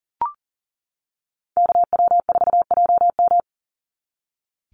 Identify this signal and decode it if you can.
Morse code, heard as a keyed tone, about 40 wpm, 700 Hz. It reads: KW4JM
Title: NA SOTA Chaser Callsigns 40wpm